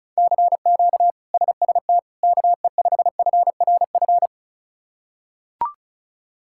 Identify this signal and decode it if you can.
Morse code, heard as a keyed tone, about 36 wpm, 700 Hz. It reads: CQ SST KE5FRF
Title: K1USN SST QSO Practice - 35wpm